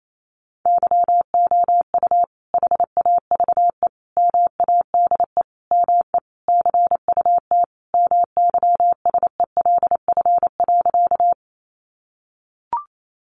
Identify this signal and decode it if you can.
Morse code, heard as a keyed tone, about 28 wpm, 700 Hz. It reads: YOU HAVE MADE ME CUT MYSELF.
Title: Sentences from Top 1300 Words 28wpm